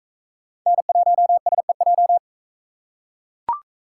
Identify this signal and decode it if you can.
Morse code, heard as a keyed tone, about 41 wpm, 700 Hz. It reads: N1SEJ